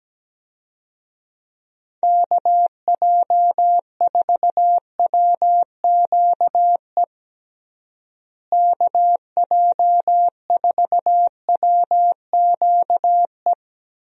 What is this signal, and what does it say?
Morse code, heard as a keyed tone, about 17 wpm, 700 Hz. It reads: KJ4WQE KJ4WQE
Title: USA Call Signs - 6 Characters - Repeated 2x - 17wpm